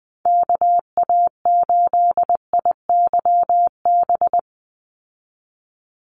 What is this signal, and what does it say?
Morse code, heard as a keyed tone, about 20 wpm, 700 Hz. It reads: KA8IYB
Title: USA Call Signs 20wpm